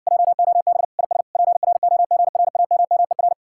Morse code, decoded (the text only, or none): JOB IS PROGRAMMER